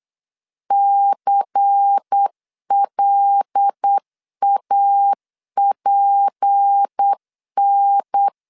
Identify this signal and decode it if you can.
Morse code, heard as a keyed tone, about 8 wpm, 800 Hz. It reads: CLAPN
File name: 8wpm